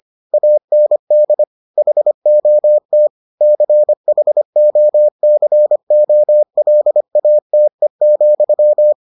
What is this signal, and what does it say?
Morse code, heard as a keyed tone, about 25 wpm, 600 Hz. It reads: AND HOT CHOCOLATE,